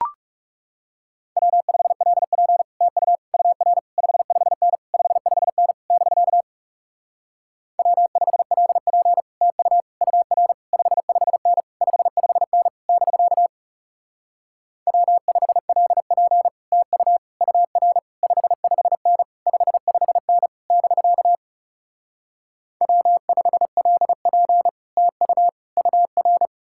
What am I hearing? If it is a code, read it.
Morse code, heard as a keyed tone, about 38 wpm, 700 Hz. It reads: W5LP TU UR 55N 55N <BK> W5LP TU UR 55N 55N <BK> W5LP TU UR 55N 55N <BK> W5LP TU UR